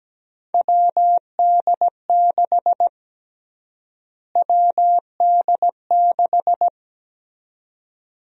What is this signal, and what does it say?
Morse code, heard as a keyed tone, about 17 wpm, 700 Hz. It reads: WD6 WD6
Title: USA Call Sign Prefixes - Repeated 2x - 17wpm